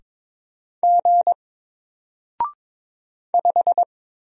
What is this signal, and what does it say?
Morse code, heard as a keyed tone, about 22 wpm, 700 Hz. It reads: G 5
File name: Single-Letter-Number-22wpm